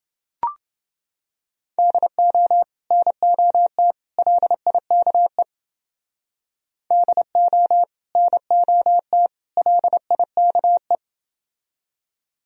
Morse code, heard as a keyed tone, about 29 wpm, 700 Hz. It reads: DO NOT LIKE DO NOT LIKE